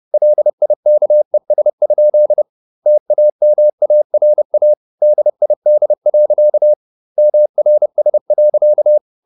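Morse code, heard as a keyed tone, about 30 wpm, 600 Hz. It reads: LIKES? TAMARA DID. MRS.
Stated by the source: Morse code at 30 wpm, the novel Never by Ken Follett in English, from CW med Gnister.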